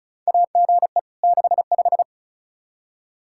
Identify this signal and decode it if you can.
Morse code, heard as a keyed tone, about 34 wpm, 700 Hz. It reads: AGE 65
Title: K1USN SST QSO Practice 35wpm